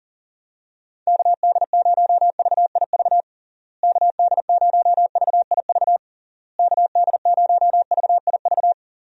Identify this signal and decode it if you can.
Morse code, heard as a keyed tone, about 40 wpm, 700 Hz. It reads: KD0VIV KD0VIV KD0VIV
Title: USA Call Signs - 6 Characters - Repeated 3x - 40wpm